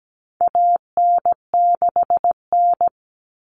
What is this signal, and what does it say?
Morse code, heard as a keyed tone, about 17 wpm, 700 Hz. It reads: AN6N
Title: Common Cut Numbers - 4-Digit 17wpm